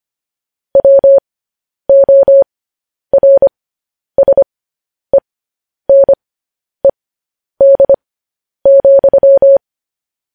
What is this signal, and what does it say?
Morse code, heard as a keyed tone, about 25 wpm, spaced as for 10 wpm, 550 Hz. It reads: WORSENED,
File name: podcast / News headlines-1x-25x10